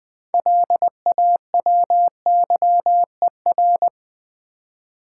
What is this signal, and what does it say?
Morse code, heard as a keyed tone, about 21 wpm, 700 Hz. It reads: LAWYER
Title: 6 Letter Words - First 3 Letters Given - 20wpm